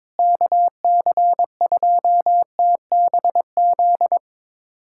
Morse code, heard as a keyed tone, about 22 wpm, 700 Hz. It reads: KC2TBZ
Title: USA Callsign DE Callsign - 6 Characters - 22wpm